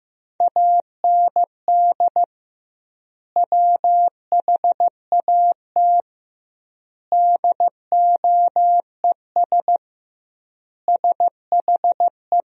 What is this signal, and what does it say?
Morse code, heard as a keyed tone, about 15 wpm, 700 Hz. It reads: AND WHAT DOES SHE
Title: 2x Word-Spacing - Sentences from Top 400 Words - 15wpm